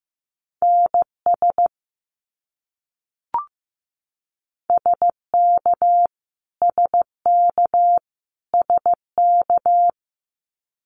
Morse code, heard as a keyed tone, about 15 wpm, 700 Hz. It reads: NS SK SK SK
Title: Canadian Provinces and Territories - Repeated 3x - 15wpm